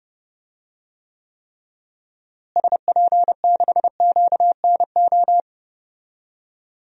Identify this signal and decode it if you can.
Morse code, heard as a keyed tone, about 30 wpm, 700 Hz. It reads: SP6QNO
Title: DXCC-100 Call Signs - 6 Characters - Repeated 3x - 30wpm